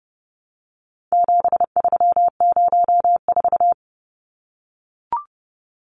Morse code, heard as a keyed tone, about 30 wpm, 700 Hz. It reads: 7304
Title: Numbers - 4-Digit 30wpm